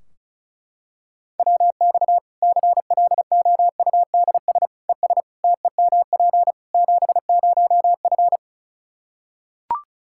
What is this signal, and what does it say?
Morse code, heard as a keyed tone, about 36 wpm, 700 Hz. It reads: WX CLOUDS ES TEMP 70F
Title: Practice QSO - Exchange 2 - Info - 35wpm